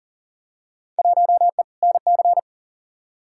Morse code, heard as a keyed tone, about 40 wpm, 700 Hz. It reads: 1E NC
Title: Field Day Exchanges - 40wpm